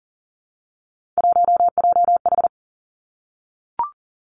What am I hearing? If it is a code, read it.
Morse code, heard as a keyed tone, about 38 wpm, 700 Hz. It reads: 1JH
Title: Three Challenging Letter-Number Combinations 40wpm